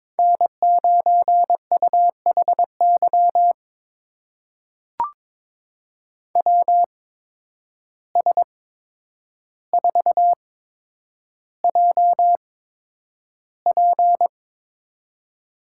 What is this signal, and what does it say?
Morse code, heard as a keyed tone, about 22 wpm, spaced as for 9 wpm, 700 Hz. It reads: N9UHY WS4JP
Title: USA Call Signs - 5 Characters - 8x Character Spacing - 22wpm